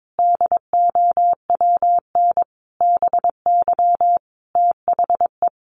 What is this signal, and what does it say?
Morse code, heard as a keyed tone, about 22 wpm, 700 Hz. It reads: DOWN BY THE
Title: Sentences from Top 1200 Words 22wpm